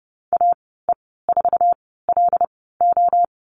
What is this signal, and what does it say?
Morse code, heard as a keyed tone, about 30 wpm, spaced as for 17 wpm, 700 Hz. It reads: AE4LO
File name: Lesson 14g - Number 2 - Call Signs 30wpm